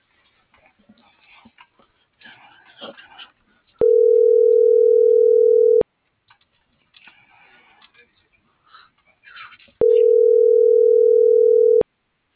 Ringback tone (North America), 440 + 480 Hz.